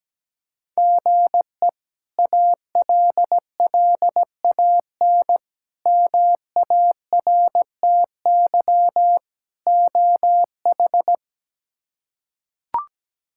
Morse code, heard as a keyed tone, about 17 wpm, 700 Hz. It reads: GE ALLAN MARTY OH
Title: K1USN SST QSO Practice - 17wpm